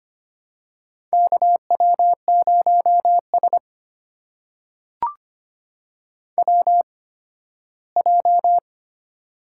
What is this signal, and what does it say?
Morse code, heard as a keyed tone, about 25 wpm, spaced as for 8 wpm, 700 Hz. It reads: KW0SWJ
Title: USA Call Signs - 8x Character Spacing - 25wpm